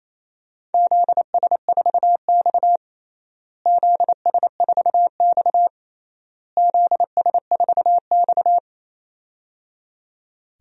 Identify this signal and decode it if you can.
Morse code, heard as a keyed tone, about 28 wpm, 700 Hz. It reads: ZS4X ZS4X ZS4X
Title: DXCC-100 Call Signs - 4 Characters - Repeated 3x - 28wpm